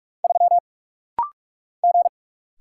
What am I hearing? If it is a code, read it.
Morse code, heard as a keyed tone, about 43 wpm, 700 Hz. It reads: 3 G
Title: Single Letter-Number - Rapid-Fire 45wpm